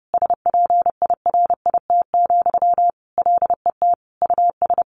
Morse code, heard as a keyed tone, about 30 wpm, 700 Hz. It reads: SPIRIT, LET US
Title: Sentences from Top 1400 Words 30wpm